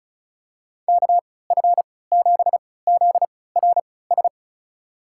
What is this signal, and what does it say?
Morse code, heard as a keyed tone, about 34 wpm, spaced as for 20 wpm, 700 Hz. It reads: KF7ZRS
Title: USA Call Signs - 6 Characters - 3x Character Spacing - 35wpm